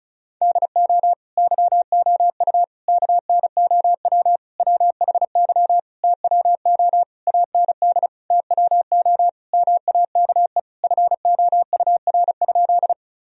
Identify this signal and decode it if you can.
Morse code, heard as a keyed tone, about 34 wpm, 700 Hz. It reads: DO YOU KNOW WHY TWO AND TWO MAKE FOUR?
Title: Sentences from Top 1000 Words - Review Part 1 - 35wpm